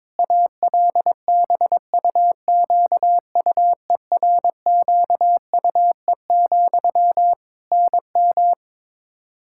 Morse code, heard as a keyed tone, about 22 wpm, 700 Hz. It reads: ALBUQUERQUE, NM
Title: Practice QSO - Exchange 1 - Beginning - 22wpm